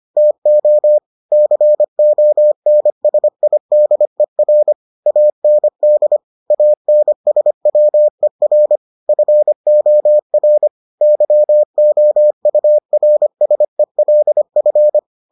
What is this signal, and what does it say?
Morse code, heard as a keyed tone, about 25 wpm, 600 Hz. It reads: TO CONSIDER AND ANSWER FOR YOURSELF